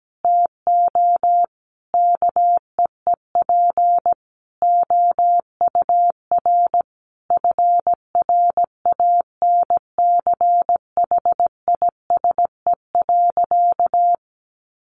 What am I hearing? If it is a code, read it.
Morse code, heard as a keyed tone, about 17 wpm, 700 Hz. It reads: TO KEEP OUR FRANCHISE.
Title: A Sound Of Thunder 17wpm